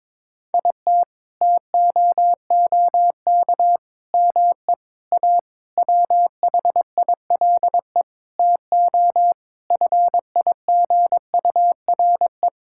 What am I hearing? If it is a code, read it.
Morse code, heard as a keyed tone, about 22 wpm, 700 Hz. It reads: IT TOOK ME A WHILE TO FIGURE
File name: practice / 22wpm / Sentences with 8-Word Phrases - 22wpm